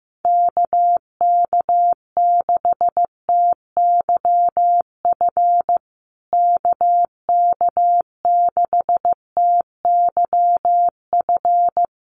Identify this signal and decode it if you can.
Morse code, heard as a keyed tone, about 15 wpm, 700 Hz. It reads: KK6TYF KK6TYF
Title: USA Call Signs Repeated 2x 15wpm